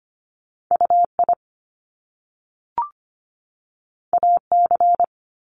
Morse code, heard as a keyed tone, about 25 wpm, 700 Hz. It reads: UI AC
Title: Top Two Letter Combinations in Words 25wpm